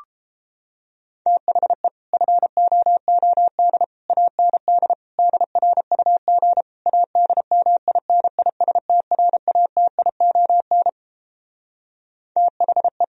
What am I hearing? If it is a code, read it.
Morse code, heard as a keyed tone, about 31 wpm, 700 Hz. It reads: THE FOOD AND DRUG ADMINISTRATION THE